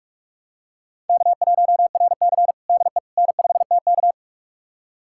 Morse code, heard as a keyed tone, about 42 wpm, 700 Hz. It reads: K1RC DE N5TK
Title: USA Callsign DE Callsign - 4 Characters - 45wpm